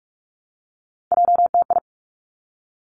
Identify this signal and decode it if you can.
Morse code, heard as a keyed tone, about 46 wpm, 700 Hz. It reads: JTI
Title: Three Challenging Letter-Number Combinations 45wpm